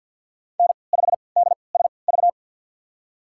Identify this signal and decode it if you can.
Morse code, heard as a keyed tone, about 52 wpm, spaced as for 28 wpm, 700 Hz. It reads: N5DSV
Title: USA Call Signs - 5 Characters - 3x Character Spacing - 50wpm